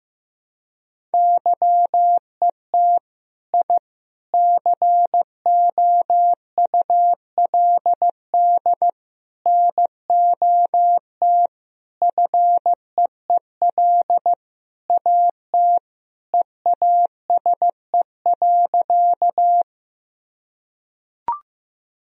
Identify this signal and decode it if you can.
Morse code, heard as a keyed tone, about 15 wpm, 700 Hz. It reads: YET I COULD NOT FEEL AT EASE.
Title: Sentences from Top 1000 Words - Review Part 1 - 15wpm